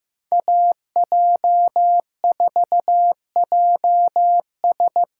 Morse code, heard as a keyed tone, about 15 wpm, 700 Hz. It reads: AJ4JS
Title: USA Call Signs - 5 Characters - 8x Character Spacing - 15wpm